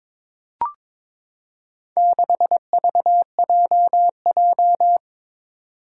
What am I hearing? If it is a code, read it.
Morse code, heard as a keyed tone, about 21 wpm, 700 Hz. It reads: 6VJJ
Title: Four Challenging Letter-Number Combinations 22wpm